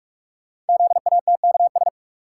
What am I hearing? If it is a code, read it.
Morse code, heard as a keyed tone, about 44 wpm, 700 Hz. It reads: ZATKS
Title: Code Groups - 5 Letters - 45wpm